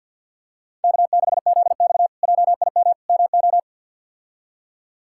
Morse code, heard as a keyed tone, about 48 wpm, 690 Hz. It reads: K6ZX JIM MO
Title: K1USN SST QSO Practice - 50wpm